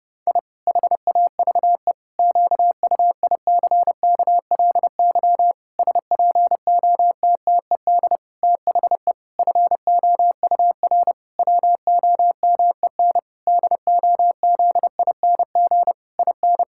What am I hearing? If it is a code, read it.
Morse code, heard as a keyed tone, about 30 wpm, 700 Hz. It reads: I HAVE QUICKLY SPOTTED THE FOUR WOMEN DOZING IN
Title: Pangram Challenge v2 - 30wpm